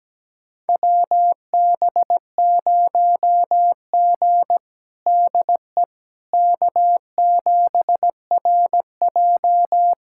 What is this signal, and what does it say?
Morse code, heard as a keyed tone, about 17 wpm, 700 Hz. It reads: WB0G DE K7RJ
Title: USA Callsign DE Callsign - 4 Characters - 17wpm